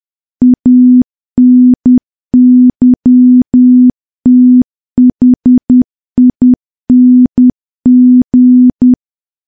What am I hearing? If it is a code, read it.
Morse code, heard as a keyed tone, about 10 wpm, 250 Hz. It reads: ANYTHING
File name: morse / morse-training-10WPM